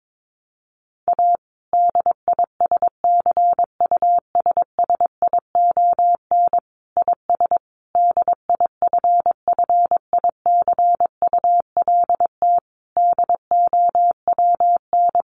Morse code, heard as a keyed tone, about 22 wpm, 700 Hz. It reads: A DISCUSSION IS DIFFICULT DOWN
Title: Sentences from Top 1100 Words 22wpm